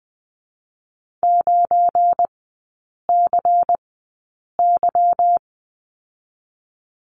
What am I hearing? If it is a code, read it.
Morse code, heard as a keyed tone, about 20 wpm, spaced as for 8 wpm, 690 Hz. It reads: 9CY
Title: 212 Lesson 34e - Number 8 - Three Characters 20wpm